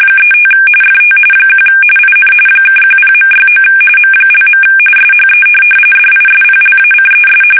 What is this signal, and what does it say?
RTTY, about 45.45 baud. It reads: 35 FT 1 IN (10.69 M). SHE DISPLACES
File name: fsk850 / USS-Massachusetts